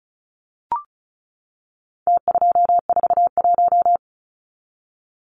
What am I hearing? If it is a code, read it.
Morse code, heard as a keyed tone, about 34 wpm, 700 Hz. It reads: T241
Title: Numbers - Contest Style - 4-Digit 35wpm